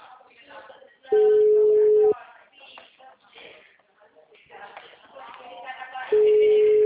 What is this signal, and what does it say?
Ringback tone (Europe, 425 Hz), with speech.
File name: out- -2013- - - .55186